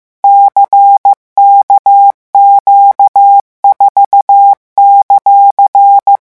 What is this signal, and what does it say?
Morse code, heard as a keyed tone, about 15 wpm, 800 Hz. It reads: CKQ4;
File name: morse mp3 / revision14-vitesse 14 mots